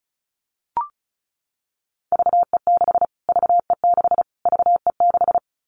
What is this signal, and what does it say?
Morse code, heard as a keyed tone, about 34 wpm, 700 Hz. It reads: VE6 VE6 VE6
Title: Callsign Prefixes for Top 25 DXCC - Repeated 3x - 35wpm